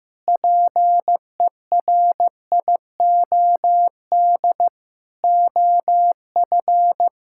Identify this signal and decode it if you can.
Morse code, heard as a keyed tone, about 15 wpm, 700 Hz. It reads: PERIOD OF
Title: Sentences from Top 1000 Words - Review Part 5 - 15wpm